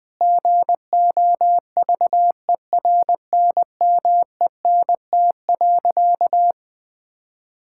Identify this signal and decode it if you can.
Morse code, heard as a keyed tone, about 20 wpm, 700 Hz. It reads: GOVERNMENT.
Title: Sentences from Top 1000 Words - Review Part 6 - 20wpm